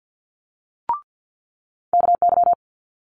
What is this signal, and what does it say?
Morse code, heard as a keyed tone, about 50 wpm, 700 Hz. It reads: KY
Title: Multi-letter Phonemes Quick Fire 50wpm